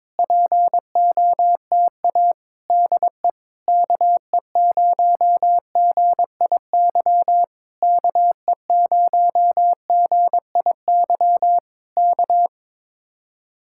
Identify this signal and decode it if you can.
Morse code, heard as a keyed tone, about 21 wpm, 700 Hz. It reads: POTA DE KE0GIY KE0GIY K
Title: Speed-Racing Common QSO Phrases Encore - 22wpm